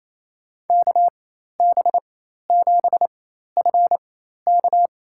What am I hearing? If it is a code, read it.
Morse code, heard as a keyed tone, about 28 wpm, spaced as for 13 wpm, 700 Hz. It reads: KB7FK
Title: USA Call Signs - 5 Characters - 4x Character Spacing - 28wpm